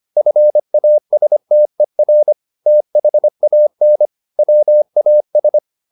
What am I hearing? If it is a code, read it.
Morse code, heard as a keyed tone, about 25 wpm, 600 Hz. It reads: FASTER THAN WAS